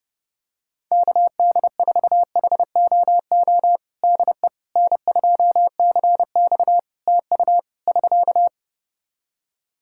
Morse code, heard as a keyed tone, about 32 wpm, 700 Hz. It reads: KD4HOO DE N2CX TU <SK>
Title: Practice QSO - Exchange 4 - Ending - 30wpm